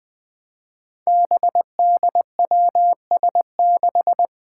Morse code, heard as a keyed tone, about 20 wpm, 700 Hz. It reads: BDWS6
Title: Five Challenging Letter-Number Combinations 20wpm